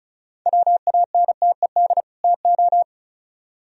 Morse code, heard as a keyed tone, about 45 wpm, 700 Hz. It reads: WANTED TO